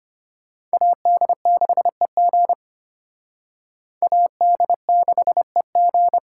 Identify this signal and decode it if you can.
Morse code, heard as a keyed tone, about 31 wpm, 700 Hz. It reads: AD6EG AD6EG
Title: Speed Racing - USA 2x2 Call Signs 25wpm